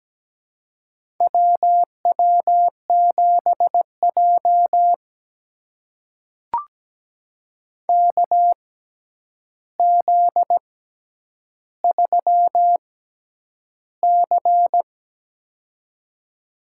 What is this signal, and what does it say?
Morse code, heard as a keyed tone, about 17 wpm, spaced as for 9 wpm, 700 Hz. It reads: WW7J KZ3C
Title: USA Call Signs - 4 Characters - 6x Character Spacing - 17wpm